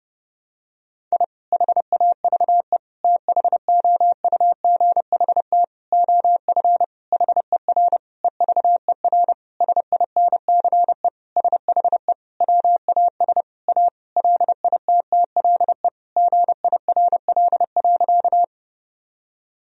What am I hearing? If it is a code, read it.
Morse code, heard as a keyed tone, about 32 wpm, 700 Hz. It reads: I HAVE THOUGHT OF HER EVER SINCE SHE WAS A LITTLE GIRL.
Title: Sentences from Top 1000 Words - Review Part 1 - 30wpm